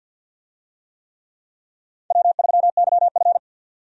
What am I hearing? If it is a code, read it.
Morse code, heard as a keyed tone, about 52 wpm, 700 Hz. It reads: W3YF
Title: USA Call Signs - 4 Characters - 50wpm